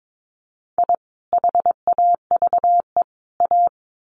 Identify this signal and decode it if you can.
Morse code, heard as a keyed tone, about 22 wpm, 700 Hz. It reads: I HAVE A
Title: Sets of 3 Words 22wpm